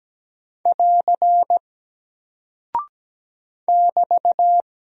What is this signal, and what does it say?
Morse code, heard as a keyed tone, about 17 wpm, 700 Hz. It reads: <AR> <BT>
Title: Single Character (Letter:Number:Punctuation:Prosign) Alt 17wpm